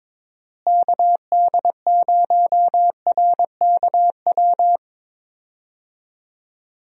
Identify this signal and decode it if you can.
Morse code, heard as a keyed tone, about 22 wpm, 700 Hz. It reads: KD0RKW
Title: USA Call Signs - 6 Characters - Repeated 3x - 22wpm